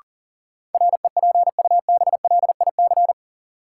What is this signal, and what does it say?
Morse code, heard as a keyed tone, about 38 wpm, 700 Hz. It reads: REPUBLIC
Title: Sentences from Top 1500 Words - New Word Drill - 40wpm